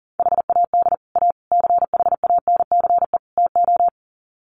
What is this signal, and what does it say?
Morse code, heard as a keyed tone, about 40 wpm, 700 Hz. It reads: HAD A CHANCE TO
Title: Sets of 4 Words 40wpm